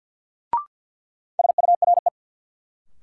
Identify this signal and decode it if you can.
Morse code, heard as a keyed tone, about 53 wpm, 700 Hz. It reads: SURE
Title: Sentences from Top 300 Words - New Word Drill - 50wpm